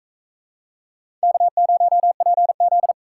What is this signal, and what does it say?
Morse code, heard as a keyed tone, about 41 wpm, 700 Hz. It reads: K0PZ